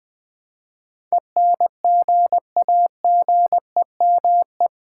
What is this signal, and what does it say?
Morse code, heard as a keyed tone, about 20 wpm, 700 Hz. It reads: ENGAGEME
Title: Build A Word - 4 Letters At A Time - 10-Letter Words - 20wpm